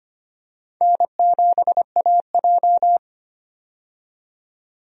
Morse code, heard as a keyed tone, about 25 wpm, 700 Hz. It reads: N7AJ
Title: USA Call Signs - 4 Characters - 25wpm